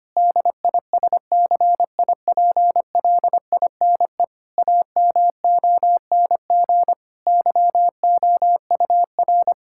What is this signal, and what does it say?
Morse code, heard as a keyed tone, about 25 wpm, 700 Hz. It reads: DISCIPLINE AMONG YOUR
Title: Sentences from Top 1000 Words - Review Part 6 - 25wpm